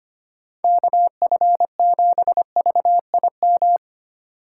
Morse code, heard as a keyed tone, about 25 wpm, 700 Hz. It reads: KF7VIM
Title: USA Call Signs - 6 Characters - 3x Character Spacing - 25wpm